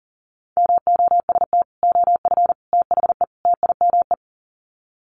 Morse code, heard as a keyed tone, about 39 wpm, 700 Hz. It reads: MOST OF THE TIME
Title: Sets of 4 Words 40wpm